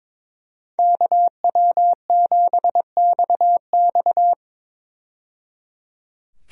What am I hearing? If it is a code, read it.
Morse code, heard as a keyed tone, about 22 wpm, 700 Hz. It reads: KW7XX
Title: USA Call Signs - 5 Characters - 22wpm